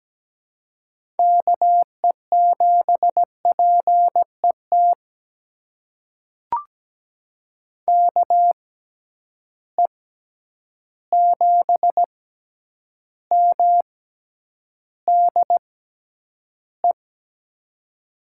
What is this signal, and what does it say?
Morse code, heard as a keyed tone, about 17 wpm, spaced as for 9 wpm, 700 Hz. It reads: KE7PET KE7MDE
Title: USA Call Signs - 6 Characters - 6x Character Spacing - 17wpm